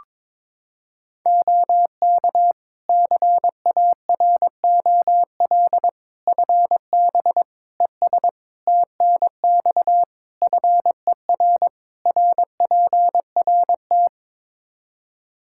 Morse code, heard as a keyed tone, about 22 wpm, 700 Hz. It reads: OK CAROL FB ES TNX FER RPRT
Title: K1USN SST QSO Practice 22wpm